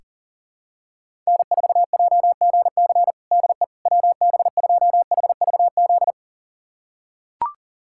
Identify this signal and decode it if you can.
Morse code, heard as a keyed tone, about 40 wpm, 700 Hz. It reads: N4JGC DE WB2HVZ
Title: USA Callsign DE Callsign 40wpm